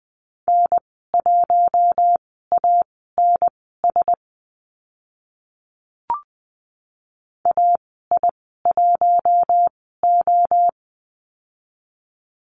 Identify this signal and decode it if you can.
Morse code, heard as a keyed tone, about 20 wpm, spaced as for 14 wpm, 700 Hz. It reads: N1ANS AI1O
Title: 048 Lesson 9g - Number 4 - Call Signs 20wpm